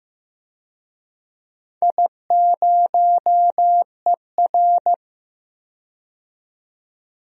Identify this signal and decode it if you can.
Morse code, heard as a keyed tone, about 15 wpm, 700 Hz. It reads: I0ER
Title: DXCC-100 Call Signs - 4 Characters - 15wpm